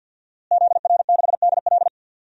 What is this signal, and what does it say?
Morse code, heard as a keyed tone, about 48 wpm, 700 Hz. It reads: ZR6DL
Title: DXCC-100 Call Signs - 5 Characters - Repeated 3x - 50wpm